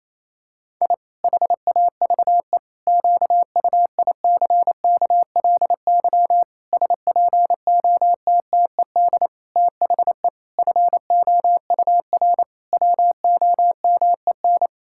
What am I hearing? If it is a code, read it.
Morse code, heard as a keyed tone, about 27 wpm, 700 Hz. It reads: I HAVE QUICKLY SPOTTED THE FOUR WOMEN
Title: Pangram Challenge v2 - 28wpm